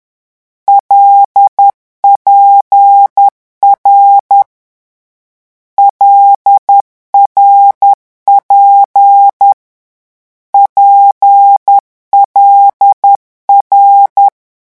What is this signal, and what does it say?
Morse code, heard as a keyed tone, about 10 wpm, 800 Hz. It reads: LPR LRP PLR
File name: morse mp3 / lecon6-vitesse 10